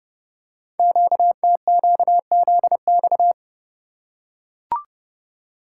Morse code, heard as a keyed tone, about 30 wpm, 690 Hz. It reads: QTQZX
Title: Code Groups - 5 Letters - 30wpm